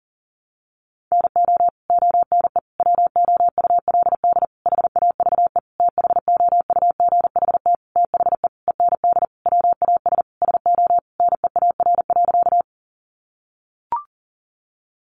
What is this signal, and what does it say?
Morse code, heard as a keyed tone, about 40 wpm, 700 Hz. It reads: NO ONE WOULD HAVE THOUGHT THE END WAS SO NEAR.
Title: Sentences from Top 300 Words 40wpm